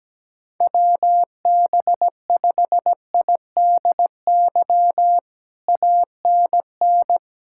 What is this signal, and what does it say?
Morse code, heard as a keyed tone, about 17 wpm, 700 Hz. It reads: WB5IDY ANN
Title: K1USN SST QSO Practice - 17wpm